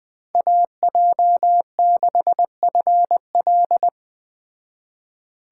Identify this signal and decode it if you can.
Morse code, heard as a keyed tone, about 20 wpm, 700 Hz. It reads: AJ6FL
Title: Speed Racing - USA 2x2 Call Signs 17wpm